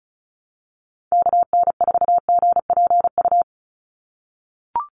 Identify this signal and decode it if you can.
Morse code, heard as a keyed tone, about 34 wpm, 700 Hz. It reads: KN4GPU
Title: USA Call Signs 35wpm